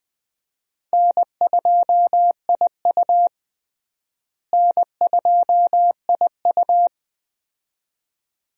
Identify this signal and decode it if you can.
Morse code, heard as a keyed tone, about 20 wpm, 700 Hz. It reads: N2IU N2IU
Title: USA Call Signs - 4 Characters - Repeated 2x - 20wpm